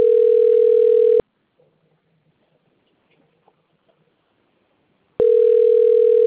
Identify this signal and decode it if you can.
Ringback tone (North America), 440 + 480 Hz.